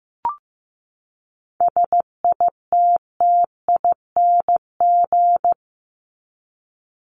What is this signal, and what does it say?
Morse code, heard as a keyed tone, about 15 wpm, 700 Hz. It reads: SITTING
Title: ING Suffix - Top 100 Words with Base Words 15wpm